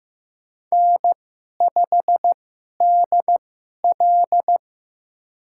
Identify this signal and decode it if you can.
Morse code, heard as a keyed tone, about 15 wpm, spaced as for 11 wpm, 700 Hz. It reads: N5DL
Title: USA Call Signs - 2x Character Spacing - 15wpm